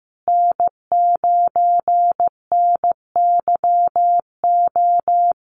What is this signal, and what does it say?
Morse code, heard as a keyed tone, about 15 wpm, 700 Hz. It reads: N9NYO
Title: USA Call Signs Repeated 2x 15wpm